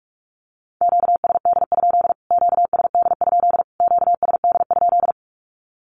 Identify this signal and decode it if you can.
Morse code, heard as a keyed tone, about 44 wpm, 700 Hz. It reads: QSD? QSD? QSD?
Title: Q-Codes Repeated 3x 45wpm